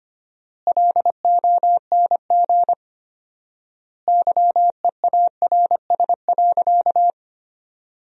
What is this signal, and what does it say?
Morse code, heard as a keyed tone, about 25 wpm, 700 Hz. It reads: LONG YEARS.
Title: 4x Word-Spacing - Sentences from Top 200 Words - 25wpm